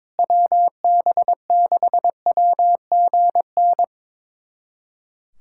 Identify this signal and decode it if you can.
Morse code, heard as a keyed tone, about 22 wpm, 700 Hz. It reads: WB6WGN
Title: USA Callsign DE Callsign - 6 Characters - 22wpm